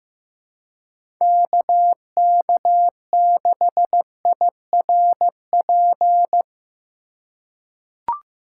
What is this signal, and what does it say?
Morse code, heard as a keyed tone, about 15 wpm, 700 Hz. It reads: KK6IRP
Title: USA Call Signs - 6 Characters - 4x Character Spacing - 15wpm